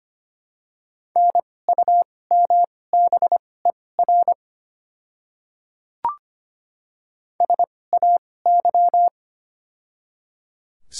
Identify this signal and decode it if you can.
Morse code, heard as a keyed tone, about 25 wpm, spaced as for 18 wpm, 690 Hz. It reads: NUMBER SAY
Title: 2x Character Spacing - Top 200 Words - 25wpm